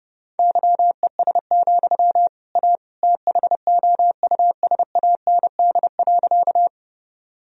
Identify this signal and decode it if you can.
Morse code, heard as a keyed tone, about 32 wpm, 700 Hz. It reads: YES, A THOUSAND.
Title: Sentences from Top 1000 Words - Review Part 3 - 30wpm